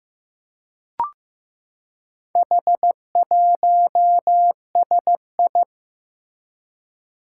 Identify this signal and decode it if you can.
Morse code, heard as a keyed tone, about 15 wpm, 700 Hz. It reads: H1SI
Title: Four Challenging Letter-Number Combinations 15wpm